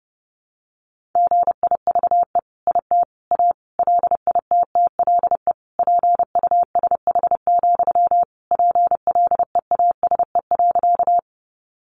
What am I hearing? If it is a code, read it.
Morse code, heard as a keyed tone, about 28 wpm, 700 Hz. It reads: GIVE IT A LITTLE PUSH, PLEASE.
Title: Sentences from Top 1500 Words 30wpm